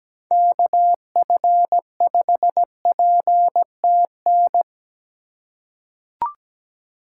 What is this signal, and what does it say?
Morse code, heard as a keyed tone, about 17 wpm, 700 Hz. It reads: KF5PTN
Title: USA Call Signs - 6 Characters - 2x Character Spacing - 17wpm